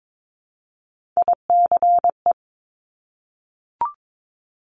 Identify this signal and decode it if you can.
Morse code, heard as a keyed tone, about 22 wpm, 700 Hz. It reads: ICE
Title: Multi-letter Phonemes Repeated 3x 22wpm